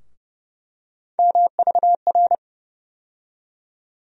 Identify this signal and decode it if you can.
Morse code, heard as a keyed tone, about 30 wpm, 700 Hz. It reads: MVR
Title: USA Call Sign Postfixes - 3 Letters 30wpm